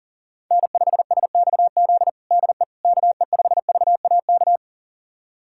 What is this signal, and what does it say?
Morse code, heard as a keyed tone, about 37 wpm, 700 Hz. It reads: N5SXZ DE KE5VAK
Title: USA Callsign DE Callsign 40wpm